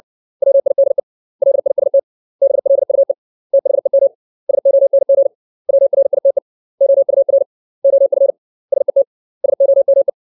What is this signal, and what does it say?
Morse code, heard as a keyed tone, about 60 wpm, 550 Hz. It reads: WIDE LIST BLUE THEM SONG WRITE OUR OF SET SOME
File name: 632 words / 632 Words 60WPM